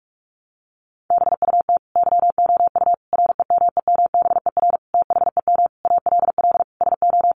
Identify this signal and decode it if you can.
Morse code, heard as a keyed tone, about 44 wpm, 700 Hz. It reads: BUT YOU REMEMBER THEM ALL SO
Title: Sentences from Top 1300 Words 45wpm